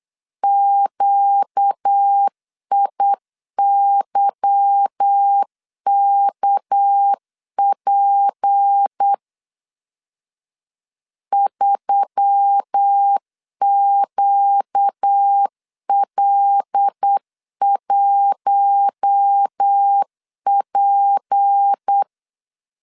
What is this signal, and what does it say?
Morse code, heard as a keyed tone, about 8 wpm, 800 Hz. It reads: QIYKP 3QL1P
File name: LEZIONI / 8 YJ45 / 8 YJ 8wpm 3313